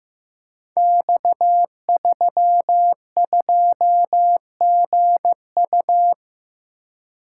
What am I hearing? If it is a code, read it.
Morse code, heard as a keyed tone, about 15 wpm, 700 Hz. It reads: X32GU
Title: Code Groups - 5 Letter:Numbers - 15wpm